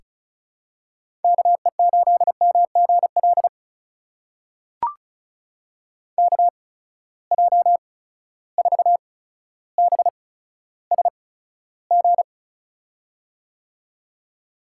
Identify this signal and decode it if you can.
Morse code, heard as a keyed tone, about 36 wpm, spaced as for 15 wpm, 700 Hz. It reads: KE8MGL KJ4BSG
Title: USA Call Signs - 6 Characters - 8x Character Spacing - 35wpm